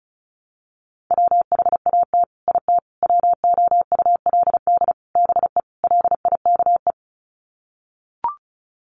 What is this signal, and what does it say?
Morse code, heard as a keyed tone, about 34 wpm, 700 Hz. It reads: WHAT IT WOULD BE LIKE
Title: Sets of 5 Words Encore 35wpm